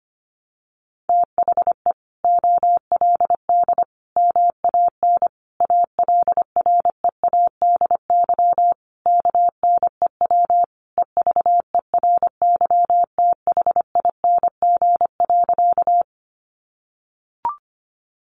Morse code, heard as a keyed tone, about 25 wpm, 700 Hz. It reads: THE OLD MAN ALREADY KNEW EVERYTHING.
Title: Sentences from Top 700 Words 25wpm